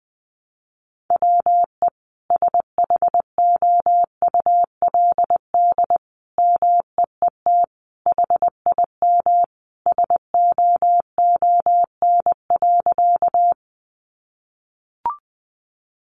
Morse code, heard as a keyed tone, about 20 wpm, 700 Hz. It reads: WE SHOULD MEET HIM SOON.
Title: Sentences from Top 1100 Words 20wpm